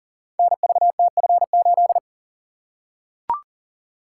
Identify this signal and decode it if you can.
Morse code, heard as a keyed tone, about 40 wpm, 700 Hz. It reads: NVTF8
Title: Code Groups - 5 Letter:Numbers - 40wpm